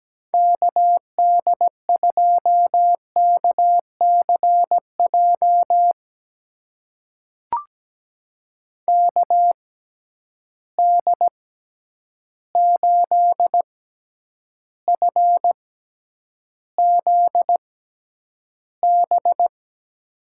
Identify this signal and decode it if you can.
Morse code, heard as a keyed tone, about 17 wpm, spaced as for 9 wpm, 700 Hz. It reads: KD2KCJ KD8FZB
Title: USA Call Signs - 6 Characters - 6x Character Spacing - 17wpm